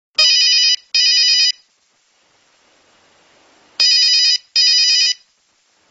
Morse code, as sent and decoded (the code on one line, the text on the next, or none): none